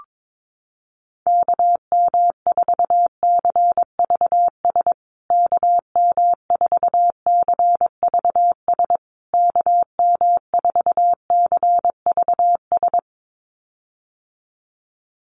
Morse code, heard as a keyed tone, about 22 wpm, 700 Hz. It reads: KM4CVS KM4CVS KM4CVS
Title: USA Call Signs Repeated 3x 22wpm